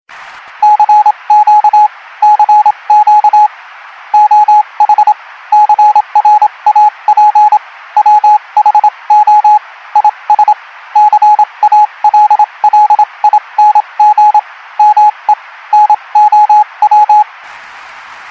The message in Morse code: CQ CQ OH CRAP WHO IS CALLING ME NOW